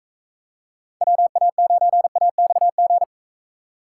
Morse code, heard as a keyed tone, about 42 wpm, 700 Hz. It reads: WA9AXG